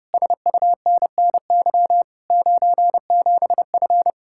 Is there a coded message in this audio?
Yes: Morse code from 0.1 to 4.1 s, heard as a keyed tone, about 33 wpm, 700 Hz. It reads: SUNNY 97F